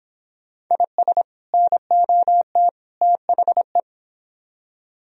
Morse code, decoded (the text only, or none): IS NOT THE